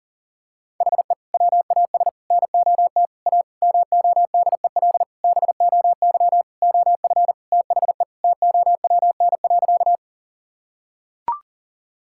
Morse code, heard as a keyed tone, about 40 wpm, 700 Hz. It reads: HE WAS NOT A MODEL BOY OF THE TOWN.
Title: Sentences from Top 1000 Words - Review Part 5 - 40wpm